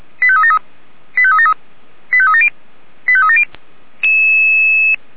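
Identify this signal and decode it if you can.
Five-tone selective call (ZVEI), heard as quick runs of five tones, tones of about 70 ms. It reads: 85373 85373 85370 85370
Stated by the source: ZVEI five-tone selective call received on SDR.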